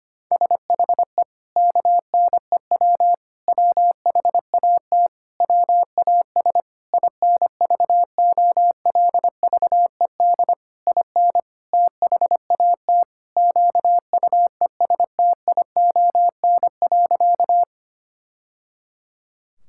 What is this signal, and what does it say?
Morse code, heard as a keyed tone, about 25 wpm, 700 Hz. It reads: SHE KNEW WHAT WAS INVOLVED IN THAT QUESTION.
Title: Sentences from Top 1000 Words - Review Part 5 - 25wpm